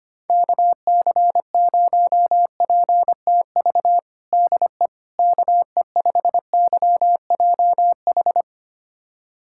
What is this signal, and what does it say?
Morse code, heard as a keyed tone, about 25 wpm, 700 Hz. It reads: KC0PTV DE KE5YJH
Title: USA Callsign DE Callsign 25wpm